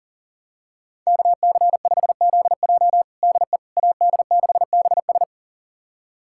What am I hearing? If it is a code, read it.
Morse code, heard as a keyed tone, about 37 wpm, 700 Hz. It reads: KC5ZJ DE AD6BS
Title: USA Callsign DE Callsign - 5 Characters - 40wpm